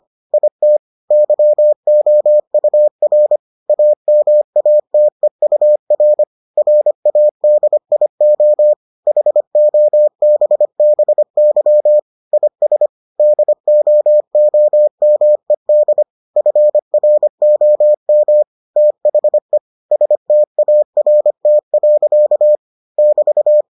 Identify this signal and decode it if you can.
Morse code, heard as a keyed tone, about 25 wpm, 600 Hz. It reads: IT YOUR AMATEUR RADIO HOBBY IS DOOMED FROM THE START. <BT>